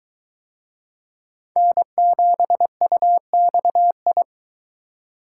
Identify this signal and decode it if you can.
Morse code, heard as a keyed tone, about 21 wpm, 700 Hz. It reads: N7UXI